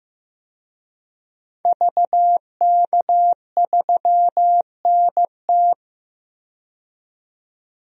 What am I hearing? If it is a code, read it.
Morse code, heard as a keyed tone, about 15 wpm, 700 Hz. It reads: VK3NT
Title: DXCC-100 Call Signs - 5 Characters - 15wpm